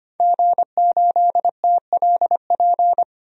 Morse code, heard as a keyed tone, about 25 wpm, 700 Hz. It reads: G8TLP
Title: DXCC-100 Call Signs - 5 Characters - 25wpm